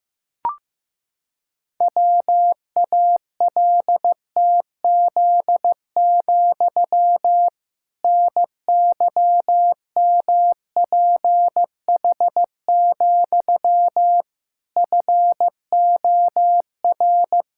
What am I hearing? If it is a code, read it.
Morse code, heard as a keyed tone, about 15 wpm, 700 Hz. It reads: WALTZ, NYMPH, FOR
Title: Pangram Challenge v2 - 15wpm